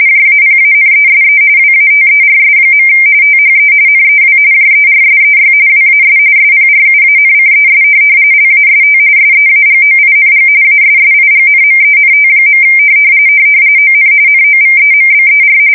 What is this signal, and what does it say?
RTTY, about 45.45 baud. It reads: JAPANESE WATERS. THE NEW JERSEY WAS SPRUANCES LAST SEAGOING ASSIGNMENT. ON NOVEMBER
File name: fsk170 / USS-New-Jersey